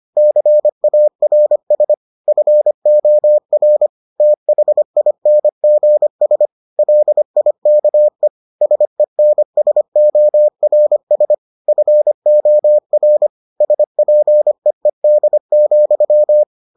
Morse code, heard as a keyed tone, about 25 wpm, 600 Hz. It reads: CARS FOR THINGS LIKE SENSORS FOR SPEED,